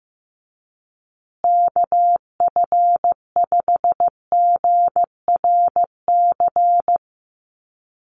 Morse code, heard as a keyed tone, about 15 wpm, 700 Hz. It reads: KF5GRC
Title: USA Call Signs Repeated 2x 15wpm